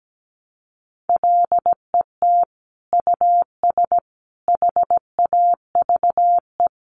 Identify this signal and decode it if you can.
Morse code, heard as a keyed tone, about 17 wpm, 700 Hz. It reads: LET US HAVE
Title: Sentences from Top 900 Words 17wpm